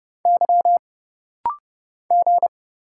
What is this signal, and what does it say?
Morse code, heard as a keyed tone, about 30 wpm, 700 Hz. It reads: Y G
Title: Single Letter-Number - Rapid-Fire 30wpm